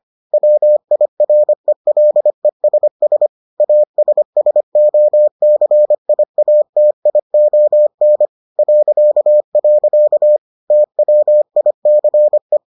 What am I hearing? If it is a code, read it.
Morse code, heard as a keyed tone, about 25 wpm, 600 Hz. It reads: WIRELESS ASSOCIATION .. TWICE